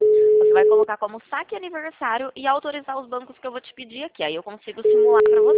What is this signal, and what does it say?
Ringback tone (Europe, 425 Hz), with speech.